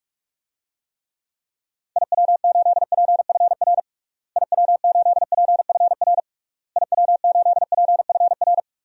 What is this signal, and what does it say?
Morse code, heard as a keyed tone, about 46 wpm, 700 Hz. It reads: IW8PFR IW8PFR IW8PFR
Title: DXCC-100 Call Signs - 6 Characters - Repeated 3x - 45wpm